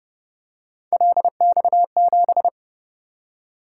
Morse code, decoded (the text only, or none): LX7